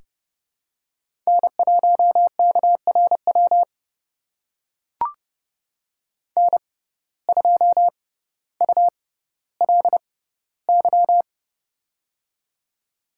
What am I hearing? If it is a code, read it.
Morse code, heard as a keyed tone, about 30 wpm, spaced as for 15 wpm, 700 Hz. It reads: N1KRW N2ULY
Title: USA Call Signs - 5 Characters - 6x Character Spacing - 30wpm